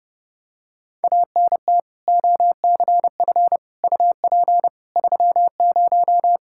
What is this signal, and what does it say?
Morse code, heard as a keyed tone, about 30 wpm, 700 Hz. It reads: ANT OCF UP 30
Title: Practice QSO - Exchange 2 - Info - 30wpm